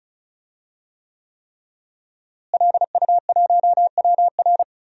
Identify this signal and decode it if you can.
Morse code, heard as a keyed tone, about 36 wpm, 700 Hz. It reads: LU1WR
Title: DXCC-100 Call Signs - 5 Characters - Repeated 3x - 35wpm